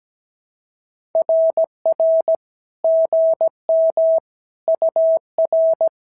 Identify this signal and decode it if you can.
Morse code, heard as a keyed tone, about 17 wpm, 650 Hz. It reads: RR GM UR
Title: SOTA QSO Practice 17wpm